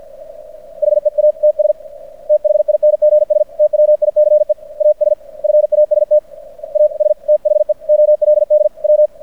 Morse code, heard as a keyed tone, about 50 wpm, 600 Hz. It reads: HEATS THINGS TWICE AS FAST AS THE OLD W
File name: archived / CRSnet